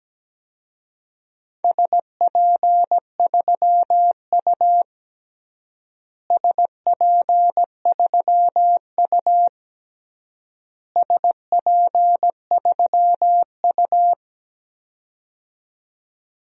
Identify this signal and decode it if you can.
Morse code, heard as a keyed tone, about 17 wpm, 700 Hz. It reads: SP3U SP3U SP3U
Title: DXCC-100 Call Signs - 4 Characters - Repeated 3x - 17wpm